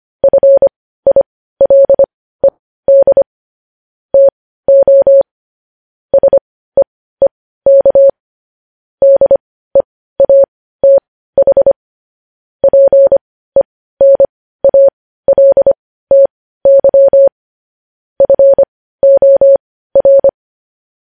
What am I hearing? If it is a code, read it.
Morse code, heard as a keyed tone, about 25 wpm, spaced as for 15 wpm, 550 Hz. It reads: FILED TO SEEK DEATH PENALTY FOR
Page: News headlines at 15